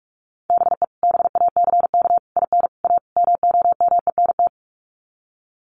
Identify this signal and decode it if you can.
Morse code, heard as a keyed tone, about 44 wpm, 700 Hz. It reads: BE BACK IN A MOMENT
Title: Sets of 5 Words 45wpm